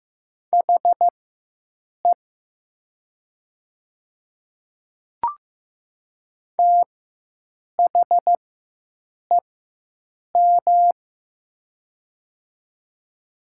Morse code, heard as a keyed tone, about 15 wpm, spaced as for 7 wpm, 700 Hz. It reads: HE THEM
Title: 4x Character Spacing - Top 100 Words - 15wpm